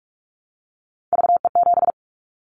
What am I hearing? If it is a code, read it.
Morse code, heard as a keyed tone, about 44 wpm, 700 Hz. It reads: VE7
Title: Callsign Prefixes for Top 100 DXCC - 45wpm